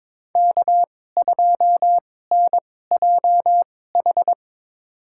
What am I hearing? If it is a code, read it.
Morse code, heard as a keyed tone, about 22 wpm, spaced as for 16 wpm, 700 Hz. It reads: K2NJH
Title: USA Call Signs - 2x Character Spacing - 22wpm